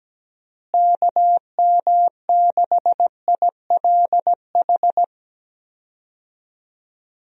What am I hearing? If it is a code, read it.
Morse code, heard as a keyed tone, about 17 wpm, 700 Hz. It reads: KM6ILH
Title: USA Call Signs - 6 Characters - Repeated 2x - 17wpm